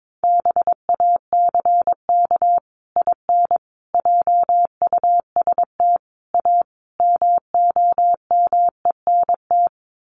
Morse code, heard as a keyed tone, about 22 wpm, 700 Hz. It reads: BACK IN JUST A MOMENT
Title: Sets of 5 Words Encore 22wpm